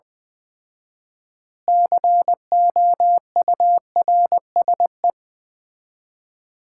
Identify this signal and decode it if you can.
Morse code, heard as a keyed tone, about 20 wpm, 700 Hz. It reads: COURSE
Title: 4x Word-Spacing - Sets of 4 Words - 20wpm